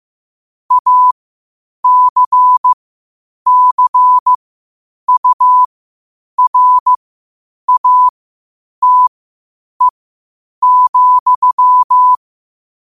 Morse code, heard as a keyed tone, about 15 wpm, spaced as for 8 wpm, 1000 Hz. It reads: ACCURATE,
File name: qotdfiles / 8wpm